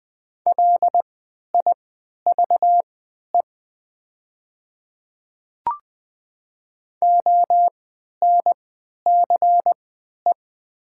Morse code, heard as a keyed tone, about 20 wpm, spaced as for 11 wpm, 700 Hz. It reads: LIVE ONCE
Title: 3x Character Spacing - Top 300 Words - 20wpm